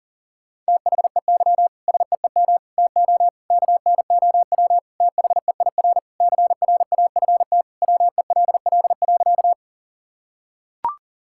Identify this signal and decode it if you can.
Morse code, heard as a keyed tone, about 40 wpm, 700 Hz. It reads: THEY SEEM TO KNOW THEIR CRAFT WELL.
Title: Sentences from Top 1000 Words - Review Part 6 - 40wpm